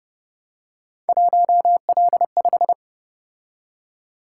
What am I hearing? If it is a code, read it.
Morse code, heard as a keyed tone, about 30 wpm, 700 Hz. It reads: 1L5
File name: Speed-Racing - Three Letter-Number Combinations - 25wpm